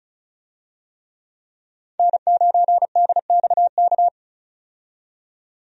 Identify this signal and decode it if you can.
Morse code, heard as a keyed tone, about 36 wpm, 700 Hz. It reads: N9DXK
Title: USA Call Signs - 5 Characters - 35wpm